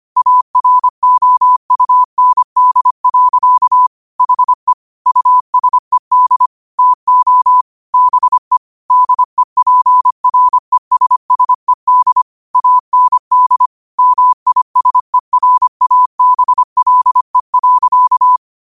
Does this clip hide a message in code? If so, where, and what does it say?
Yes: Morse code from 0.2 to 18.4 s, heard as a keyed tone, about 25 wpm, 1000 Hz. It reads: AROUND. HE USED TO BE DEPRESSED AND MISERABLE.